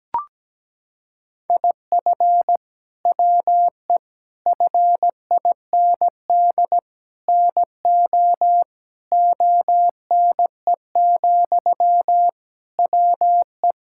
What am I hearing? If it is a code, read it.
Morse code, heard as a keyed tone, about 17 wpm, 700 Hz. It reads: IF WE FIND NO ONE, WE
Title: Sentences from Top 1600 Words 17wpm